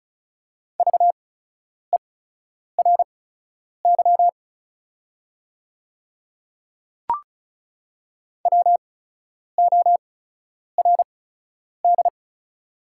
Morse code, heard as a keyed tone, about 36 wpm, spaced as for 10 wpm, 690 Hz. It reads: VERY WORD
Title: 8x Character Spacing - Top 300 Words - 35wpm